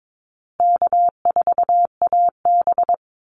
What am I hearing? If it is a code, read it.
Morse code, heard as a keyed tone, about 22 wpm, 700 Hz. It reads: K4AB
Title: Easy Call Signs Repeated 3x - 22wpm